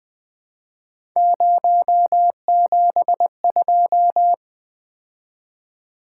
Morse code, heard as a keyed tone, about 20 wpm, 700 Hz. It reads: 072
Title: Three Letter-Number Combinations - 20wpm